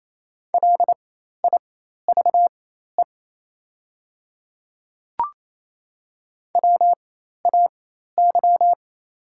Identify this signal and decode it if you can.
Morse code, heard as a keyed tone, about 28 wpm, spaced as for 13 wpm, 700 Hz. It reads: LIVE WAY
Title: 4x Character Spacing - Top 300 Words - 28wpm